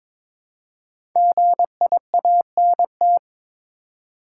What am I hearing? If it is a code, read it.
Morse code, heard as a keyed tone, about 22 wpm, 700 Hz. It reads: GIANT
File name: 5 Letter Words - First 3 Letters Given - 22wpm